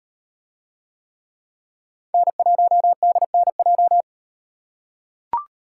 Morse code, heard as a keyed tone, about 34 wpm, 700 Hz. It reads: N1DNJ